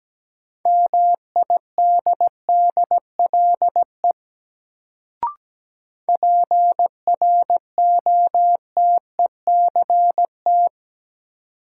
Morse code, heard as a keyed tone, about 17 wpm, 700 Hz. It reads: MIDDLE PROTECT
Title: Sentences from Top 900 Words - New Word Drill - 17wpm